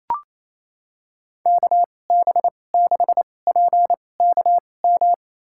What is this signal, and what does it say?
Morse code, heard as a keyed tone, about 27 wpm, spaced as for 20 wpm, 700 Hz. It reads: KB6PKM
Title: USA Call Signs - 2x Character Spacing - 28wpm